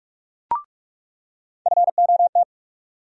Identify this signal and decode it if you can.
Morse code, heard as a keyed tone, about 46 wpm, 700 Hz. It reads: FOT
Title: USA Call Sign Postfixes - 3 Letters 45wpm